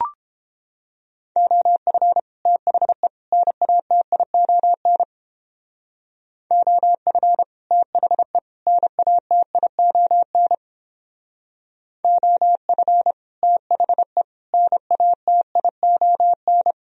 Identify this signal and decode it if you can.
Morse code, heard as a keyed tone, about 29 wpm, 700 Hz. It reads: OF THE NATION OF THE NATION OF THE NATION